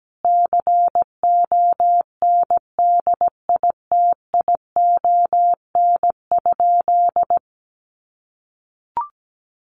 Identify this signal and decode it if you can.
Morse code, heard as a keyed tone, about 17 wpm, 700 Hz. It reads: CONDITION?
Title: Sentences from Top 1300 Words 17wpm